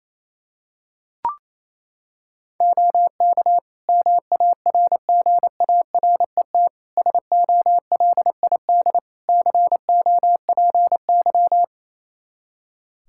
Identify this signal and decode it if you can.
Morse code, heard as a keyed tone, about 28 wpm, 700 Hz. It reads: OK MARGARET SOLID COPY
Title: Practice QSO - Exchange 3 - More Info - 28wpm